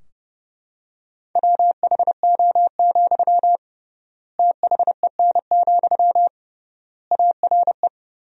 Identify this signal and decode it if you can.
Morse code, heard as a keyed tone, about 30 wpm, 700 Hz. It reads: WHO, THEN, ARE
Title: 3x Word-Spacing - Sentences from Top 100 Words - 30wpm